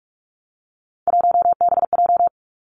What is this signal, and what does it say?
Morse code, heard as a keyed tone, about 46 wpm, 700 Hz. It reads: 1BJ
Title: Three Challenging Letter-Number Combinations 45wpm